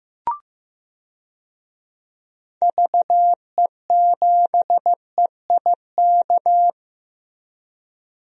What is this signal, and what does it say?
Morse code, heard as a keyed tone, about 15 wpm, 700 Hz. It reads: VE7EIK
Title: DXCC-100 Call Signs - 6 Characters - Repeated 3x - 15wpm